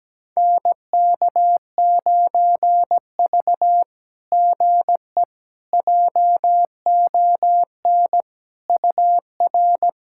Morse code, heard as a keyed tone, about 17 wpm, 700 Hz. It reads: NK9V GE JON UR
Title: SOTA QSO Practice 17wpm